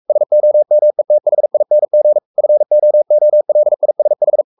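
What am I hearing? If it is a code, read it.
Morse code, heard as a keyed tone, about 44 wpm, 600 Hz. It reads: SOMETHING FOOLISH
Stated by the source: Morse code at 43 wpm sending Ken Follett's novel Never in English, from CW med Gnister.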